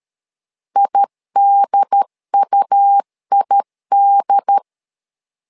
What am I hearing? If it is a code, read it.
Morse code, heard as a keyed tone, about 13 wpm, 800 Hz. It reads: IDUID
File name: LEZIONI / 2 UHDN / 2 UHDN 12wpm 1456